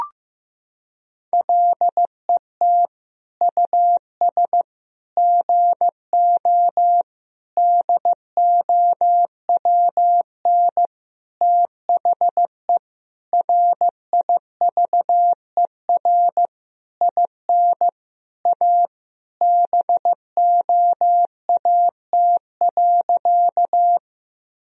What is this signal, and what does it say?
Morse code, heard as a keyed tone, about 16 wpm, 700 Hz. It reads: LET US GO DOWN THE RIVER IN A BOAT.
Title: Sentences from Top 1000 Words - Review Part 3 - 15wpm